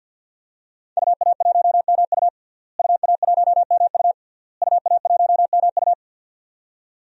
Morse code, heard as a keyed tone, about 50 wpm, 700 Hz. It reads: UA1MU UA1MU UA1MU
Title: DXCC-100 Call Signs - 5 Characters - Repeated 3x - 50wpm